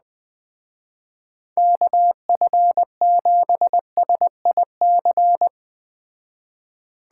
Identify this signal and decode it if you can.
Morse code, heard as a keyed tone, about 20 wpm, 700 Hz. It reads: KF7SIC